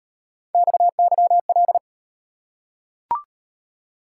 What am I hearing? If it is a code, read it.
Morse code, heard as a keyed tone, about 34 wpm, 700 Hz. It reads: XYL